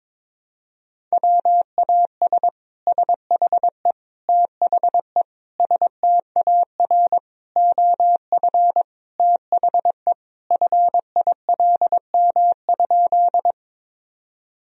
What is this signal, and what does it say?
Morse code, heard as a keyed tone, about 21 wpm, 700 Hz. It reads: WAS SHE THE STAR OF THE FILM?
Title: Sentences from Top 1000 Words - Review Part 3 - 22wpm